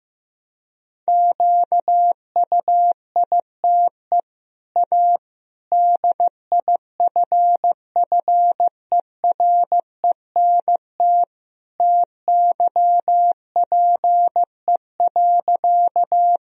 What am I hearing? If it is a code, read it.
Morse code, heard as a keyed tone, about 16 wpm, 700 Hz. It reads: QUITE A DIFFERENT TYPE.
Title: Sentences from Top 1000 Words - Review Part 3 - 15wpm